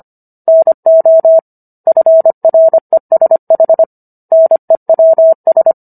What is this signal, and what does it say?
Morse code, heard as a keyed tone, about 25 wpm, 650 Hz. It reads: NO FRESH NEWS